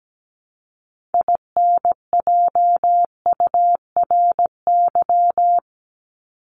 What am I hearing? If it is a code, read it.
Morse code, heard as a keyed tone, about 17 wpm, 700 Hz. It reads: INJURY
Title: Top 1751-2000 Words 17wpm